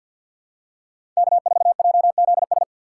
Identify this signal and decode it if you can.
Morse code, heard as a keyed tone, about 52 wpm, 700 Hz. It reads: K4JZS
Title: USA Call Signs - 5 Characters - 6x Character Spacing - 50wpm